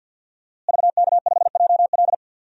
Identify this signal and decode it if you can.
Morse code, heard as a keyed tone, about 53 wpm, 700 Hz. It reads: VK5JL
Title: DXCC-100 Call Signs - 5 Characters - Repeated 3x - 50wpm